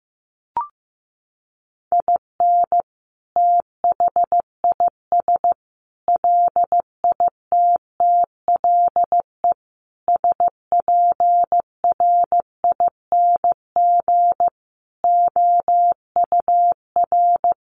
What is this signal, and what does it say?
Morse code, heard as a keyed tone, about 15 wpm, 700 Hz. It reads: IN THIS LITTLE SPRING OUR
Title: Sentences from Top 1200 Words 15wpm